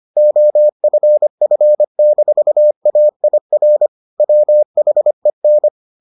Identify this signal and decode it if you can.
Morse code, heard as a keyed tone, about 25 wpm, 600 Hz. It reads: OFF-AIR WHEN